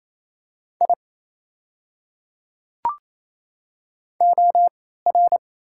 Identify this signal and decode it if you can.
Morse code, heard as a keyed tone, about 28 wpm, spaced as for 16 wpm, 690 Hz. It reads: I OR
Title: 3x Character Spacing - Top 100 Words - 28wpm